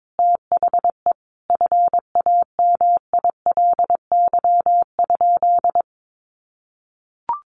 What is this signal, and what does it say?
Morse code, heard as a keyed tone, about 22 wpm, 700 Hz. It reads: THE FAMILY?
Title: Sentences from Top 1100 Words 22wpm